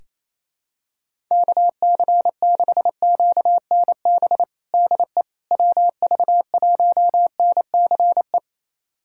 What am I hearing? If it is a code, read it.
Morse code, heard as a keyed tone, about 28 wpm, 700 Hz. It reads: KC6QNB DE WV1NCE
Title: USA Callsign DE Callsign - 6 Characters - 28wpm